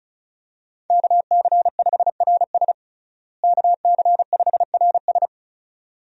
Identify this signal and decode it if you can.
Morse code, heard as a keyed tone, about 34 wpm, 700 Hz. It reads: KC5RS KC5RS
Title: USA Call Signs - 5 Characters - Repeated 2x - 35wpm